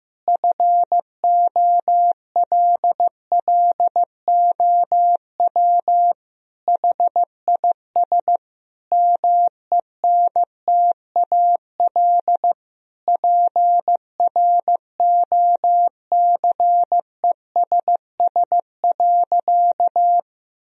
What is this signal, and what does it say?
Morse code, heard as a keyed tone, about 16 wpm, 700 Hz. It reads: FOLLOW HIS MENTAL PROCESS.
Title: Sentences from Top 1000 Words - Review Part 4 - 15wpm